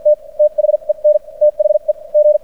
Morse code, heard as a keyed tone, about 53 wpm, spaced as for 41 wpm, 600 Hz. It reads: T THEN THE G
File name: archived / CRSnet